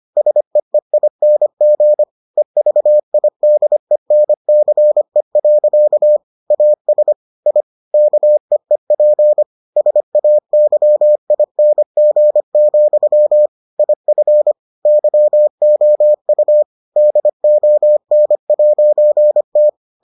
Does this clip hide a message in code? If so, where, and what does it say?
Yes: Morse code from 0.2 to 19.7 s, heard as a keyed tone, about 25 wpm, 600 Hz. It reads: SEEING EVIDENCE. AS I KEEP SAYING, IF YOU DON'T